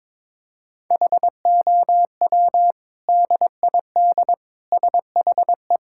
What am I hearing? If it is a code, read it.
Morse code, heard as a keyed tone, about 22 wpm, 700 Hz. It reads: HOW DID SHE
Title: Sentences from Top 1000 Words - Review Part 5 - 22wpm